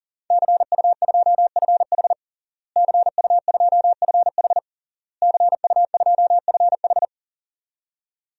Morse code, heard as a keyed tone, about 40 wpm, 700 Hz. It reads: CU2FH CU2FH CU2FH
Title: DXCC-100 Call Signs - 5 Characters - Repeated 3x - 40wpm